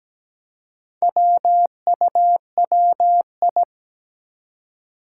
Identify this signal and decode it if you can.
Morse code, heard as a keyed tone, about 17 wpm, 700 Hz. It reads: WUWI
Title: Four Challenging Letter-Number Combinations 17wpm